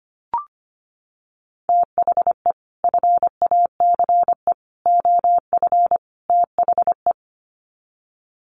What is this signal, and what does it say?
Morse code, heard as a keyed tone, about 25 wpm, 700 Hz. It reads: THE FACE OF THE
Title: Sets of 4 Words 25wpm